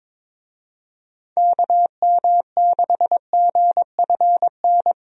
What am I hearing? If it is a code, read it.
Morse code, heard as a keyed tone, about 22 wpm, 700 Hz. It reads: KM6GFN
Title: USA Call Signs - 6 Characters - 22wpm